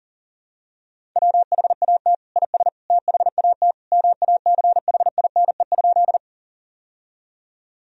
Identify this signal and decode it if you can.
Morse code, heard as a keyed tone, about 40 wpm, 700 Hz. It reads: WHAT IS THAT MACHINE?
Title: Sentences from Top 1000 Words - Review Part 2 - 40wpm